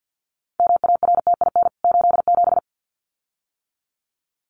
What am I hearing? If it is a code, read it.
Morse code, heard as a keyed tone, about 50 wpm, 700 Hz. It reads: MARTIN 87
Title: CWT Exchange 50wpm